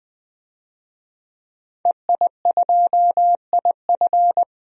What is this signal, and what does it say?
Morse code, heard as a keyed tone, about 21 wpm, 700 Hz. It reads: EI2IF
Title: DXCC-100 Call Signs - 5 Characters - 20wpm